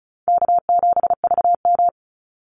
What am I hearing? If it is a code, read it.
Morse code, heard as a keyed tone, about 36 wpm, 700 Hz. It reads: K7VM
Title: Call Signs - 35wpm